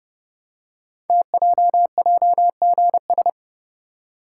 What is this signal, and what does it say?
Morse code, heard as a keyed tone, about 30 wpm, 700 Hz. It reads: TJJGS
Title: Code Groups - 5 Letters - 30wpm